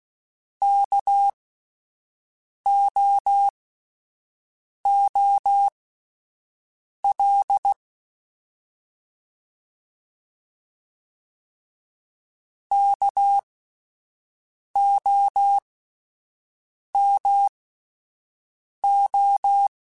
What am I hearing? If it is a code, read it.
Morse code, heard as a keyed tone, about 16 wpm, spaced as for 5 wpm, 800 Hz. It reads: KOOL KOMO
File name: Alphabetical / Alphabetical mp3s / Ks @5